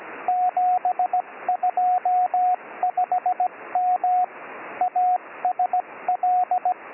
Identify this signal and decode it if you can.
Morse code, heard as a keyed tone, about 17 wpm, 730 Hz, with radio noise. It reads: 725M ASL